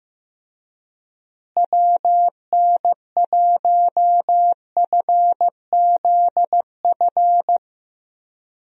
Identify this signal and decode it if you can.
Morse code, heard as a keyed tone, about 15 wpm, 700 Hz. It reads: WN1FZF
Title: USA Call Signs - 6 Characters - Repeated 2x - 15wpm